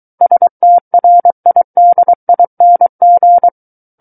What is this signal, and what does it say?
Morse code, heard as a keyed tone, about 23 wpm, 700 Hz. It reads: STRIDING
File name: Book One / 14-Chapter 14 - 23 WPM